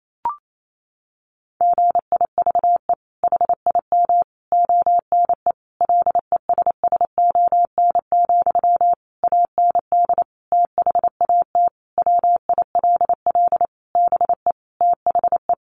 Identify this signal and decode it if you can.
Morse code, heard as a keyed tone, about 28 wpm, 700 Hz. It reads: GIVE HIM ONE LESSON, AND THAT WILL BE THE
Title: Sentences from Top 1200 Words 28wpm